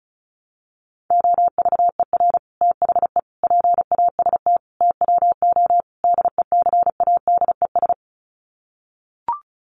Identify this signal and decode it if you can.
Morse code, heard as a keyed tone, about 34 wpm, 700 Hz. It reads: OVER THE PAST TWO DECADES
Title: Sets of 5 Words Encore2 35wpm